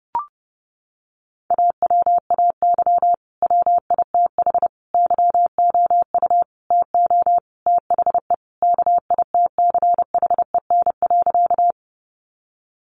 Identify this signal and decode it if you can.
Morse code, heard as a keyed tone, about 30 wpm, 700 Hz. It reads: AWAY WITH YOU TO THE KITCHEN.
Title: Sentences from Top 1400 Words 30wpm